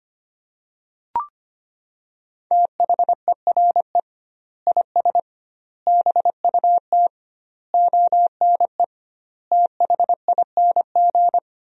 Morse code, heard as a keyed tone, about 25 wpm, 700 Hz. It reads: THERE IS BUT ONE THING
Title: 2x Word-Spacing - Sentences from Top 100 Words - 25wpm